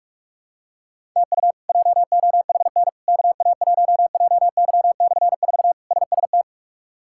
Morse code, heard as a keyed tone, about 46 wpm, 700 Hz. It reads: TU JOHN KA1JY/4 SST
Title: K1USN SST QSO Practice - 45wpm